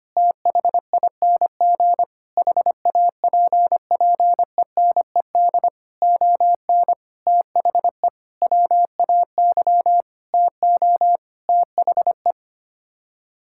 Morse code, heard as a keyed tone, about 25 wpm, 700 Hz. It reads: THING HAPPENED ON THE WAY TO THE
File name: Sets of 9 Words - 25wpm